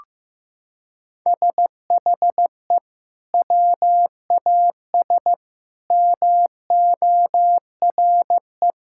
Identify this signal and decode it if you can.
Morse code, heard as a keyed tone, about 15 wpm, 700 Hz. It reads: SHE WAS MORE
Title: Sentences from Top 1600 Words 15wpm